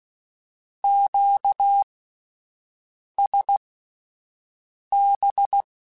Morse code, heard as a keyed tone, about 16 wpm, spaced as for 5 wpm, 800 Hz. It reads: QSB
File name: (spaced) / Q-Calls Spaced-mp3s / Q Calls sp Once @10